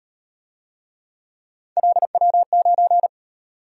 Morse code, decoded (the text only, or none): LW9